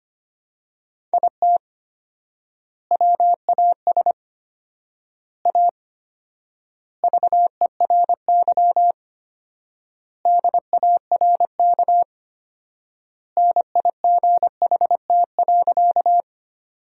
Morse code, heard as a keyed tone, about 25 wpm, 700 Hz. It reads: IT WAS A VERY DARK NIGHT.
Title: 4x Word-Spacing - Sentences from Top 500 Words - 25wpm